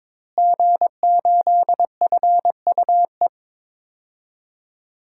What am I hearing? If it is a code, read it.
Morse code, heard as a keyed tone, about 22 wpm, 700 Hz. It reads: G8FUE
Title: DXCC-100 Call Signs - 5 Characters - Repeated 3x - 22wpm